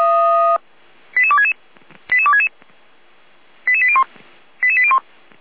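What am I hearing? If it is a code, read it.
Five-tone selective call (ZVEI), heard as quick runs of five tones, tones of about 70 ms. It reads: 88266 88266 88801 88801
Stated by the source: ZVEI five-tone selective call received on SDR, fire brigade Feuerwehr Orlamünde.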